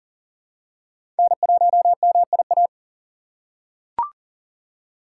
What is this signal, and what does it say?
Morse code, heard as a keyed tone, about 40 wpm, 700 Hz. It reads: N1MIA
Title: USA Call Signs - 5 Characters - 2x Character Spacing - 40wpm